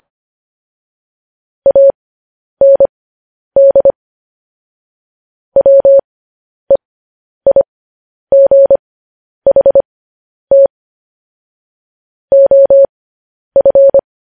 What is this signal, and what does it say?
Morse code, heard as a keyed tone, about 25 wpm, spaced as for 10 wpm, 550 Hz. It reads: AND WEIGHT OF
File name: podcast / News headlines-1x-25x10